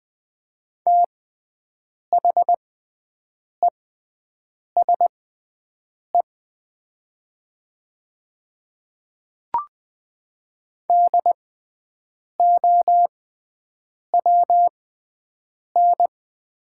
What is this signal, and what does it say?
Morse code, heard as a keyed tone, about 20 wpm, spaced as for 7 wpm, 700 Hz. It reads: THESE DOWN
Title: 6x Character Spacing - Top 200 Words - 20wpm